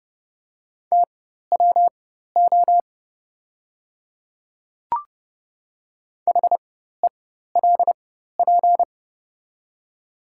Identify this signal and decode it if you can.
Morse code, heard as a keyed tone, about 31 wpm, spaced as for 14 wpm, 700 Hz. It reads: TWO HELP
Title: 4x Character Spacing - Top 200 Words - 30wpm